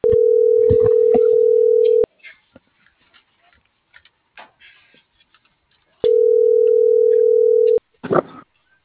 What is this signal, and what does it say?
Ringback tone (North America), 440 + 480 Hz.